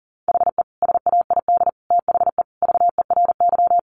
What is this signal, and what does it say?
Morse code, heard as a keyed tone, about 40 wpm, 700 Hz. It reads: HE SAID THE VERY
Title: Sentences from Top 1300 Words 40wpm